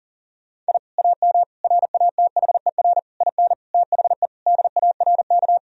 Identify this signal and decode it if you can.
Morse code, heard as a keyed tone, about 40 wpm, 700 Hz. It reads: I AM RATHER IN THE DARK
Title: Sentences from Top 1000 Words - Review Part 4 - 40wpm